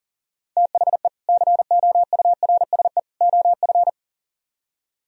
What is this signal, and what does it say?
Morse code, heard as a keyed tone, about 50 wpm, 700 Hz. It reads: THE COURSE OF
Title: Speed-Racing - Sets of 3 Words 40wpm